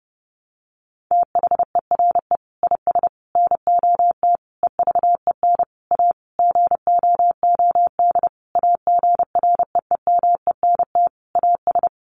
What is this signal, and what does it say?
Morse code, heard as a keyed tone, about 30 wpm, 700 Hz. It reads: THERE IS NOT EVEN A GOOD AGREEMENT AS
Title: Sentences from Top 1500 Words 30wpm